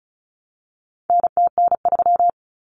Morse code, heard as a keyed tone, about 34 wpm, 700 Hz. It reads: NTN3
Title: Common Cut Numbers - 4-Digit 35wpm